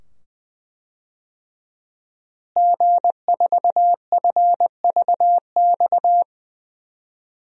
Morse code, heard as a keyed tone, about 20 wpm, 700 Hz. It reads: G4FVX
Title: DXCC-100 Call Signs - 5 Characters - Repeated 2x - 20wpm